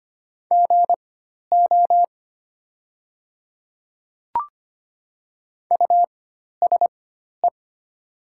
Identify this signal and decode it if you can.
Morse code, heard as a keyed tone, about 25 wpm, spaced as for 12 wpm, 700 Hz. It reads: GO USE
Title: 4x Character Spacing - Top 100 Words - 25wpm